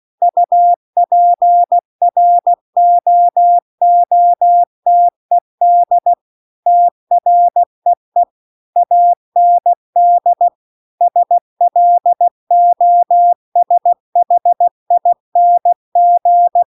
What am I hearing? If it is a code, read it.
Morse code, heard as a keyed tone, about 16 wpm, 700 Hz. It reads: UPROOTED TREE AND SLOSHING